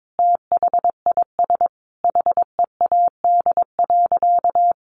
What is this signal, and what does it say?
Morse code, heard as a keyed tone, about 22 wpm, 700 Hz. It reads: THIS HEAD.
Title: Sentences from Top 1500 Words 22wpm